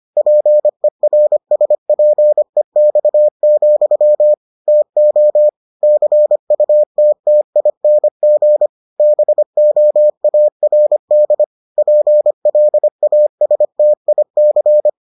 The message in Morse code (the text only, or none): PERSPEX, TO CUTTING BOARD PLASTIC